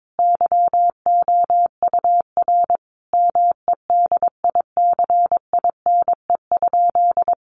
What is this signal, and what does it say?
Morse code, heard as a keyed tone, about 22 wpm, 700 Hz. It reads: YOUR MEDICINE?
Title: Sentences from Top 900 Words 22wpm